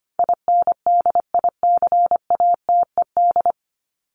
Morse code, heard as a keyed tone, about 25 wpm, 700 Hz. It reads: INDICATED
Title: ED Suffix - Top 100 Words with Base Words 25wpm